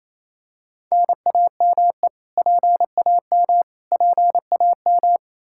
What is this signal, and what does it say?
Morse code, heard as a keyed tone, about 28 wpm, 700 Hz. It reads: NAME PAM PAM
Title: Practice QSO - Exchange 1 - Beginning - 28wpm